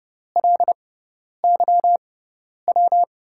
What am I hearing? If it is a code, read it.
Morse code, heard as a keyed tone, about 30 wpm, spaced as for 10 wpm, 700 Hz. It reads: LYW
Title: USA Call Signs - 6x Character Spacing - 30wpm